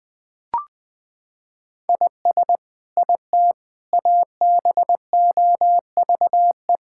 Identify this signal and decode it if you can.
Morse code, heard as a keyed tone, about 20 wpm, 700 Hz. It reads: IS IT ABOVE
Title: Sentences from Top 1000 Words - Review Part 5 - 20wpm